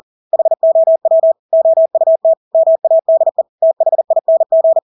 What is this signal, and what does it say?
Morse code, heard as a keyed tone, about 40 wpm, 650 Hz. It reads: HOW OUT MADE THING
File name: Engelske ord 40-40 wpm